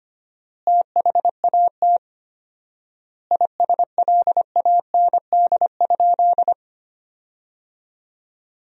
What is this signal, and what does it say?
Morse code, heard as a keyed tone, about 25 wpm, 700 Hz. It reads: THAT ISLAND?
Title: 4x Word-Spacing - Sentences from Top 500 Words - 25wpm